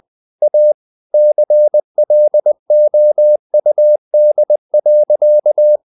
Morse code, heard as a keyed tone, about 20 wpm, 600 Hz. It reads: A CLOUD.
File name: the old man and the sea part 2 20wpm